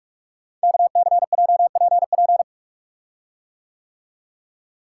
Morse code, heard as a keyed tone, about 44 wpm, 700 Hz. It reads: KCJPP
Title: Code Groups - 5 Letters - 45wpm